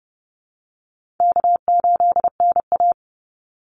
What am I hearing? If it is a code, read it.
Morse code, heard as a keyed tone, about 30 wpm, 700 Hz. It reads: K8NA
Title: Easy Call Signs - 30wpm